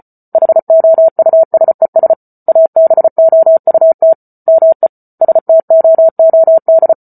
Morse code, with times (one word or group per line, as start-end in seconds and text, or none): 0.3-2.1 HOUSES
2.5-4.1 ABOUT
4.5-4.9 ME
5.2-6.9 STOOD